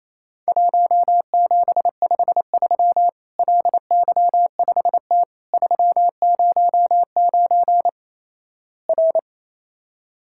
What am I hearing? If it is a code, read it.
Morse code, heard as a keyed tone, about 28 wpm, 700 Hz. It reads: 1753 LY5T 309 R
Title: Worked All Europe DX Contest - QTC Exchange - 28wpm